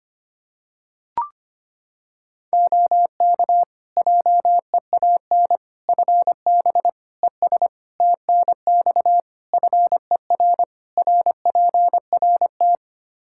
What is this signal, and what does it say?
Morse code, heard as a keyed tone, about 25 wpm, 700 Hz. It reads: OK JEAN FB ES TNX FER RPRT
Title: Practice QSO - Exchange 2 - Info - 25wpm